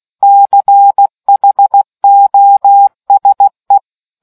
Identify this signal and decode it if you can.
Morse code, heard as a keyed tone, about 16 wpm, 800 Hz. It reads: CHOSE